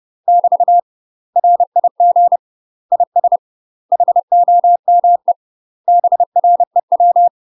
Morse code, heard as a keyed tone, about 30 wpm, 700 Hz. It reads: <BT> RIG IS HOME BREW
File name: UK MASTER QSOs TRAINING / 30WPM / UK MASTER QSOs 0038